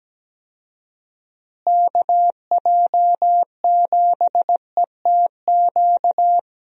Morse code, heard as a keyed tone, about 17 wpm, 700 Hz. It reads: KJ7ETQ
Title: USA Call Signs - 6 Characters - Repeated 2x - 17wpm